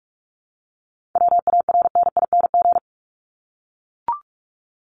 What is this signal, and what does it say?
Morse code, heard as a keyed tone, about 44 wpm, 700 Hz. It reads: WARNING
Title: Top 1751-2000 Words 45wpm